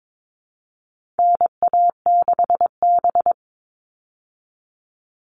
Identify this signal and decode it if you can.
Morse code, heard as a keyed tone, about 22 wpm, 700 Hz. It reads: NA6B
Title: All Cut Numbers - 4-Digit 22wpm